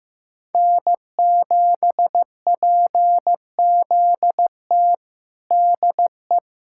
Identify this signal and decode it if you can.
Morse code, heard as a keyed tone, about 15 wpm, 700 Hz. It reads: N7PZT DE
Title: USA Callsign DE Callsign 15wpm